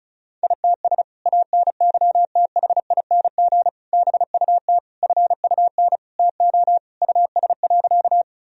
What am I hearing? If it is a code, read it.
Morse code, heard as a keyed tone, about 34 wpm, 700 Hz. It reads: ITS ANYTHING BUT FUN TO US.
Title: Sentences from Top 1000 Words - Review Part 4 - 35wpm